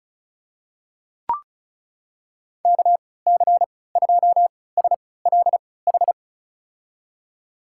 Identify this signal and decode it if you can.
Morse code, heard as a keyed tone, about 36 wpm, spaced as for 20 wpm, 700 Hz. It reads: KC2SLH
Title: USA Call Signs - 6 Characters - 3x Character Spacing - 35wpm